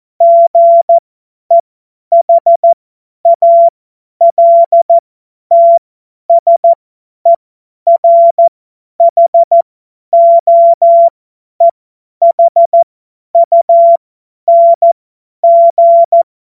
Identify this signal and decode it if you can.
Morse code, heard as a keyed tone, about 14 wpm, spaced as for 10 wpm, 680 Hz. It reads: GEHALTSERHOEHUNG
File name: Witz-22-Juli-2025-50-F70